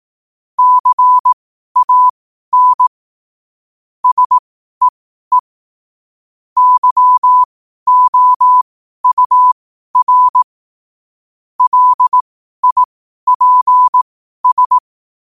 Morse code, heard as a keyed tone, about 18 wpm, spaced as for 12 wpm, 1000 Hz. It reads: CAN SEE YOUR LIPS
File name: qotdfiles / 12wpm